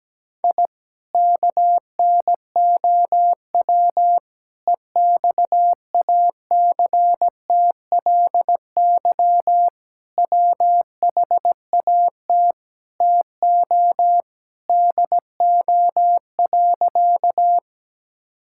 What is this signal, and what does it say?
Morse code, heard as a keyed tone, about 17 wpm, 700 Hz. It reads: I KNOW EXACTLY WHAT TO DO.
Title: Sentences from Top 1000 Words - Review Part 6 - 17wpm